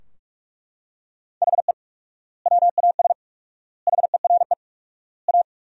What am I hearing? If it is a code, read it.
Morse code, heard as a keyed tone, about 47 wpm, 700 Hz. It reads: HE WAS HERE A
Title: 4x Word-Spacing - Sentences from Top 500 Words - 45wpm